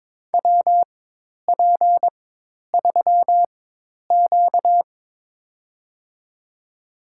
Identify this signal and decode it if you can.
Morse code, heard as a keyed tone, about 23 wpm, spaced as for 10 wpm, 700 Hz. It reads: WP3Q
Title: USA Call Signs - 4 Characters - 4x Character Spacing - 22wpm